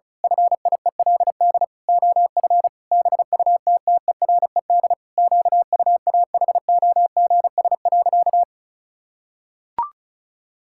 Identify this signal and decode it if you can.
Morse code, heard as a keyed tone, about 36 wpm, 700 Hz. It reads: FIELD OF BUTTERED QUAHOGS.
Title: Pangram Challenge v2 - 35wpm